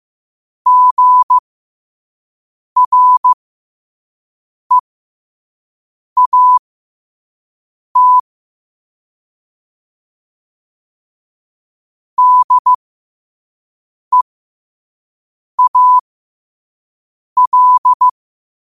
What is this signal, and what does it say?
Morse code, heard as a keyed tone, about 15 wpm, spaced as for 5 wpm, 1000 Hz. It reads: GREAT DEAL
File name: qotdfiles / 5wpm